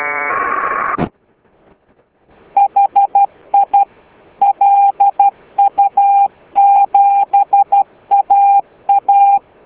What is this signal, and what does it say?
Morse code, heard as a keyed tone, about 12 wpm, 800 Hz. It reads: HI LU7AA